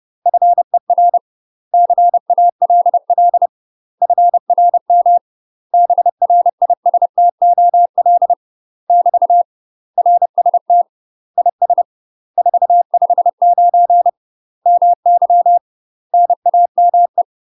Morse code, heard as a keyed tone, about 30 wpm, 700 Hz. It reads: FER CALL FRM BRISTOL <BT> RST IS 459 MY NAME
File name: UK MASTER QSOs TRAINING / 30WPM / UK MASTER QSOs 0014